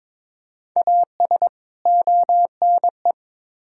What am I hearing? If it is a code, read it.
Morse code, heard as a keyed tone, about 22 wpm, 700 Hz. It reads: AS ONE
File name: Build A Phrase - 5-Word Phrases - 22wpm